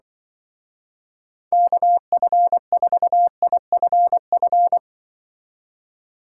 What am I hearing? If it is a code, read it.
Morse code, heard as a keyed tone, about 24 wpm, 700 Hz. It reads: KF4IFF
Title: Speed Racing - USA Call Signs 20wpm